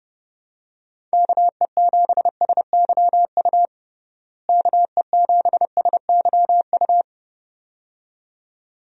Morse code, heard as a keyed tone, about 30 wpm, 700 Hz. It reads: KE7SYU KE7SYU
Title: USA Call Signs - 6 Characters - Repeated 2x - 30wpm